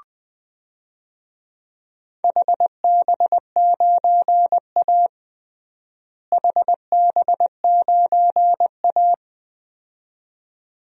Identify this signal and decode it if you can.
Morse code, heard as a keyed tone, about 20 wpm, 700 Hz. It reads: HB9A HB9A
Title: DXCC-100 Call Signs - 4 Characters - Repeated 2x - 20wpm